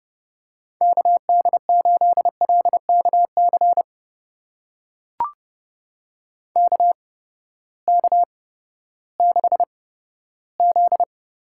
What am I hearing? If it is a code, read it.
Morse code, heard as a keyed tone, about 28 wpm, spaced as for 15 wpm, 700 Hz. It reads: KD8LKC KK6Z
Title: USA Call Signs - 6 Characters - 8x Character Spacing - 30wpm